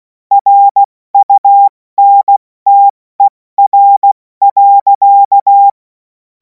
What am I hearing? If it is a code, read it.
Morse code, heard as a keyed tone, about 16 wpm, spaced as for 14 wpm, 800 Hz. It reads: RUNTER.
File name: Morsekurs / Witze / Witz 07. April 2026 70 F80